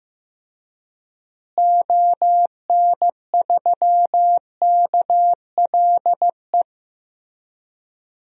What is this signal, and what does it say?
Morse code, heard as a keyed tone, about 15 wpm, 700 Hz. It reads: ON3KLE
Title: DXCC-100 Call Signs - 6 Characters - Repeated 2x - 15wpm